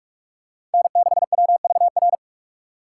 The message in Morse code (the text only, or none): N6WVR